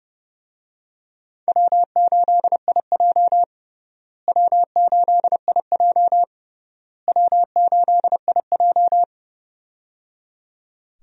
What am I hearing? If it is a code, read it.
Morse code, heard as a keyed tone, about 30 wpm, 700 Hz. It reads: W8IJ W8IJ W8IJ
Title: USA Call Signs - 4 Characters - Repeated 3x - 30wpm